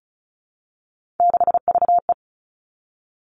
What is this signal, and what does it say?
Morse code, heard as a keyed tone, about 36 wpm, 700 Hz. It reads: 6VE
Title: Three Challenging Letter-Number Combinations 35wpm